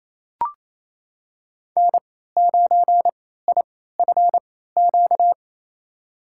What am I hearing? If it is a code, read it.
Morse code, heard as a keyed tone, about 27 wpm, spaced as for 16 wpm, 700 Hz. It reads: N9IFQ
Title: USA Call Signs - 3x Character Spacing - 28wpm